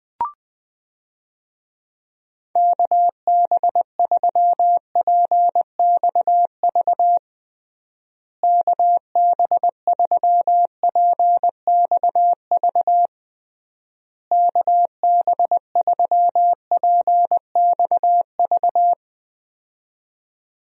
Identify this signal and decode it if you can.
Morse code, heard as a keyed tone, about 20 wpm, 700 Hz. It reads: KB3PXV KB3PXV KB3PXV
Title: USA Call Signs - 6 Characters - Repeated 3x - 20wpm